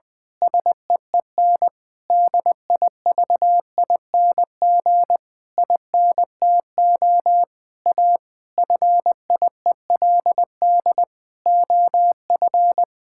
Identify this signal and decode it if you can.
Morse code, heard as a keyed tone, about 20 wpm, 700 Hz. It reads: SEEN DIVING INTO A FIELD OF
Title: Pangram Challenge v2 - 20wpm